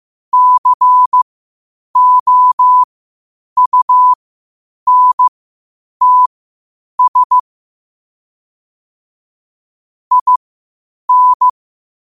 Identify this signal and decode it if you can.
Morse code, heard as a keyed tone, about 15 wpm, spaced as for 8 wpm, 1000 Hz. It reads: COUNTS IN
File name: qotdfiles / 8wpm